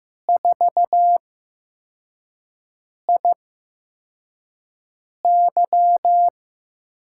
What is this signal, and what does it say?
Morse code, heard as a keyed tone, about 15 wpm, spaced as for 4 wpm, 700 Hz. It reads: 4IY
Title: USA Call Signs - 6 Characters - 8x Character Spacing - 15wpm